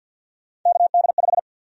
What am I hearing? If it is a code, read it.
Morse code, heard as a keyed tone, about 53 wpm, 700 Hz. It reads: KD5
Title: USA Call Sign Prefixes - Repeated 2x - 50wpm